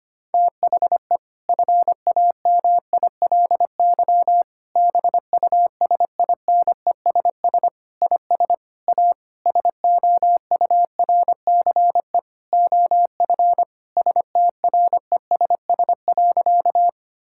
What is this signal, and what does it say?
Morse code, heard as a keyed tone, about 25 wpm, 700 Hz. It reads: THE FAMILY BUSINESS IS A SOURCE OF STRESS.
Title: Sentences from Top 1000 Words - Review Part 6 - 25wpm